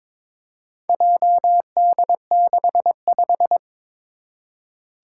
Morse code, heard as a keyed tone, about 22 wpm, 700 Hz. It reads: JD65
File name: practice / 22wpm / Four Letter-Number Combinations - 22wpm